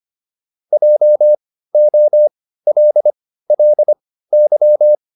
Morse code, heard as a keyed tone, about 25 wpm, spaced as for 15 wpm, 600 Hz. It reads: JOLLY